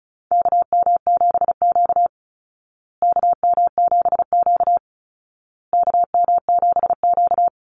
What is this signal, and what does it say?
Morse code, heard as a keyed tone, about 34 wpm, 700 Hz. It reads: KM7Q KM7Q KM7Q
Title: Call Signs Repeated 2x - 35wpm